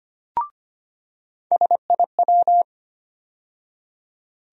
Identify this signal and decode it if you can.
Morse code, heard as a keyed tone, about 25 wpm, 700 Hz. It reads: SIW
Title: USA Call Sign Postfixes - 3 Letters 25wpm